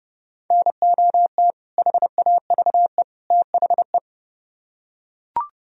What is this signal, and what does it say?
Morse code, heard as a keyed tone, about 30 wpm, 700 Hz. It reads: NOT HAVE THE